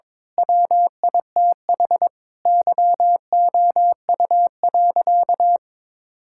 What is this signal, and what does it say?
Morse code, heard as a keyed tone, about 21 wpm, 700 Hz. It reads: WITH YOU.
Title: Sentences from Top 1600 Words 22wpm